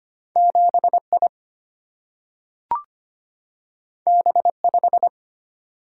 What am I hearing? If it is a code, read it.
Morse code, heard as a keyed tone, about 25 wpm, 700 Hz. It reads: 7I B5
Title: Single-Letter Single-Number 25wpm